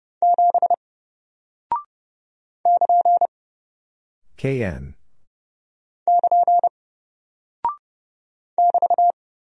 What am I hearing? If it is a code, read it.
Morse code, heard as a keyed tone, about 30 wpm, 700 Hz. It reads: 7 <KN> <KN> <BT>
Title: Single Character (Letter:Number:Punctuation:Prosign) Alt 30wpm